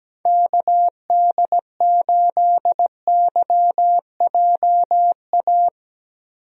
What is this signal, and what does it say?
Morse code, heard as a keyed tone, about 17 wpm, 700 Hz. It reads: KD8YJA
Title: USA Call Signs - 6 Characters - 17wpm